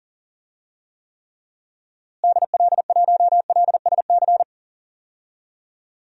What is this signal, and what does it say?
Morse code, heard as a keyed tone, about 40 wpm, 700 Hz. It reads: DL1LSC
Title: DXCC-100 Call Signs - 6 Characters - Repeated 2x - 40wpm